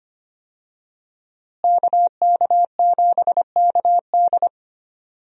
Morse code, heard as a keyed tone, about 25 wpm, 700 Hz. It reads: KK7KD
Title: USA Call Signs - 5 Characters - Repeated 3x - 25wpm